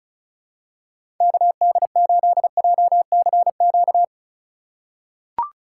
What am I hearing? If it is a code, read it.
Morse code, heard as a keyed tone, about 34 wpm, 700 Hz. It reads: KD8JCQ
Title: USA Call Signs - 6 Characters - 4x Character Spacing - 35wpm